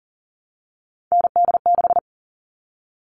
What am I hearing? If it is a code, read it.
Morse code, heard as a keyed tone, about 40 wpm, 700 Hz. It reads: ND6
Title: All Cut Numbers - 3-Digit 40wpm